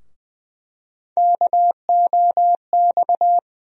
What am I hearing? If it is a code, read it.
Morse code, heard as a keyed tone, about 21 wpm, 700 Hz. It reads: KOX
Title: USA Call Sign Postfixes - 3 Letters 20wpm